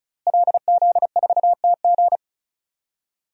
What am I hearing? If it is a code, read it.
Morse code, heard as a keyed tone, about 34 wpm, 700 Hz. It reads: LZ4TG
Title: DXCC-100 Call Signs - 5 Characters - Repeated 2x - 35wpm